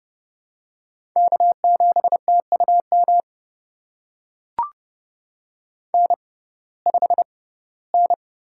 Morse code, heard as a keyed tone, about 31 wpm, spaced as for 18 wpm, 700 Hz. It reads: K7TUM N5N
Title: USA Call Signs - 5 Characters - 6x Character Spacing - 30wpm